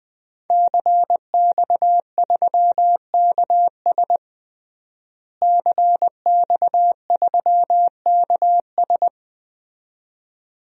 Morse code, heard as a keyed tone, about 20 wpm, 700 Hz. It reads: CX3KS CX3KS
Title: DXCC-100 Call Signs - 5 Characters - Repeated 2x - 20wpm